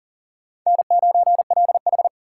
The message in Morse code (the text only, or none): N9LH